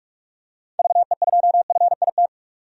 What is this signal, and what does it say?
Morse code, heard as a keyed tone, about 46 wpm, 700 Hz. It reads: VE2FIT
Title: DXCC-100 Call Signs - 6 Characters - 45wpm